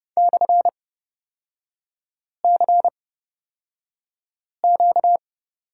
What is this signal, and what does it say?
Morse code, heard as a keyed tone, about 30 wpm, spaced as for 5 wpm, 700 Hz. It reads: /CQ
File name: Lesson 38e - Letter q - Three Characters 30wpm